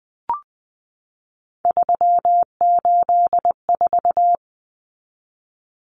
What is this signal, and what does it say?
Morse code, heard as a keyed tone, about 20 wpm, 700 Hz. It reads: 384
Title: Common Cut Numbers - 3-Digit 20wpm